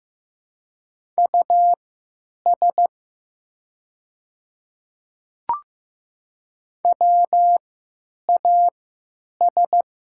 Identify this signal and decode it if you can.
Morse code, heard as a keyed tone, about 15 wpm, spaced as for 9 wpm, 700 Hz. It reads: US WAS
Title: 3x Character Spacing - Top 300 Words - 15wpm